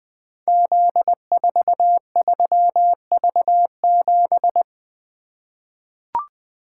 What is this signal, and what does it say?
Morse code, heard as a keyed tone, about 20 wpm, 700 Hz. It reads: Z43V7
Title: Code Groups - 5 Letter:Numbers - 20wpm